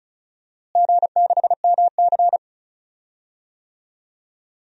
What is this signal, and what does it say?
Morse code, heard as a keyed tone, about 34 wpm, 700 Hz. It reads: G6MC
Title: DXCC-100 Call Signs - 4 Characters - Repeated 2x - 35wpm